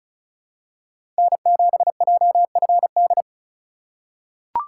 Morse code, heard as a keyed tone, about 36 wpm, 700 Hz. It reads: N7JFD
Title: USA Call Signs - 3x Character Spacing - 35wpm